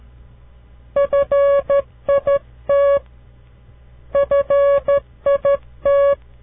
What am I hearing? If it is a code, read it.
Morse code, heard as a keyed tone, about 13 wpm, 560 Hz. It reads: FIT FIT
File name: CW Learning Programs / farnsworth / unit 2